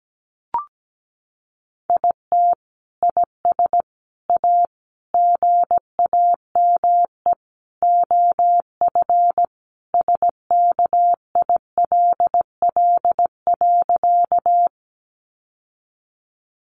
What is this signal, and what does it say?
Morse code, heard as a keyed tone, about 17 wpm, 700 Hz. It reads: IT IS A GAME OF SKILL.
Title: Sentences from Top 1200 Words 17wpm